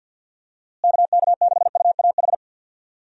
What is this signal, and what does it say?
Morse code, heard as a keyed tone, about 51 wpm, 700 Hz. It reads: KK6UAH
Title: USA Call Signs - 6 Characters - Repeated 3x - 50wpm